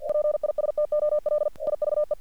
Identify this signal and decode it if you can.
Morse code, heard as a keyed tone, about 48 wpm, 600 Hz. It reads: PISTOL IVE
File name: archived / CRSnet